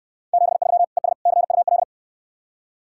Morse code, heard as a keyed tone, about 69 wpm, 700 Hz. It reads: 73 ES CUL